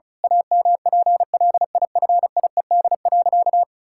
Morse code, heard as a keyed tone, about 34 wpm, 700 Hz. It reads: AMPLIFIED.
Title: Pangram Challenge v2 - 35wpm